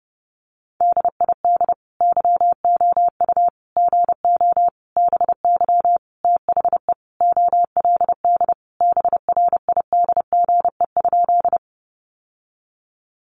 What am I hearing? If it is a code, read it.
Morse code, heard as a keyed tone, about 30 wpm, 700 Hz. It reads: DID YOU GO BY THE OLD BRIDGE?
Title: Sentences from Top 1200 Words 30wpm